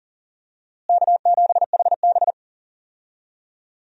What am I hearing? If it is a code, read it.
Morse code, heard as a keyed tone, about 37 wpm, 700 Hz. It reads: K7HB
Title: USA Call Signs - 4 Characters - Repeated 3x - 40wpm